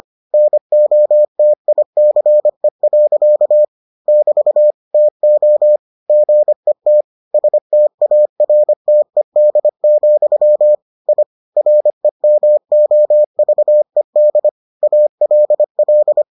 Morse code, heard as a keyed tone, about 25 wpm, 600 Hz. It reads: NOTICE. <BT> TO GET STARTED, I REMOVED ALL